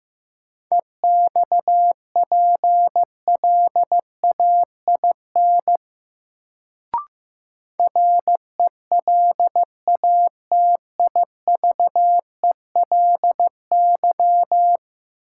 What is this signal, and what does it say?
Morse code, heard as a keyed tone, about 15 wpm, 700 Hz. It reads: EXPLAIN RELATIVELY
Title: Sentences from Top 1100 Words - New Word Drill Alt - 15wpm